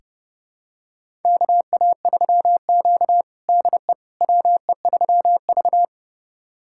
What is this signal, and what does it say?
Morse code, heard as a keyed tone, about 30 wpm, 700 Hz. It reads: KA3Q DE WE3V
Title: USA Callsign DE Callsign - 4 Characters - 30wpm